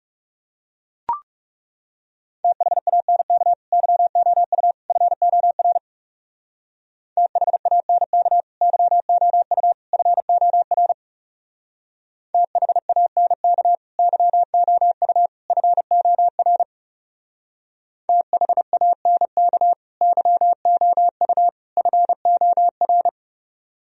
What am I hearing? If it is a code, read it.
Morse code, heard as a keyed tone, about 38 wpm, 700 Hz. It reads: THANK YOU FOR THANK YOU FOR THANK YOU FOR THANK YOU FOR